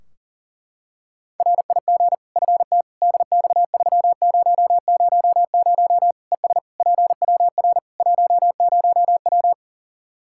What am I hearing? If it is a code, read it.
Morse code, heard as a keyed tone, about 40 wpm, 700 Hz. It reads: RIG FT DX3000 ES PWR 10W
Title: Practice QSO - Exchange 2 - Info - 40wpm